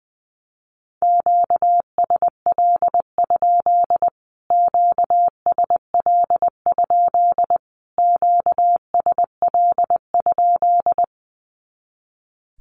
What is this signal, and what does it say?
Morse code, heard as a keyed tone, about 20 wpm, 700 Hz. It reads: QSL? QSL? QSL?
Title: Q-Codes Repeated 3x 20wpm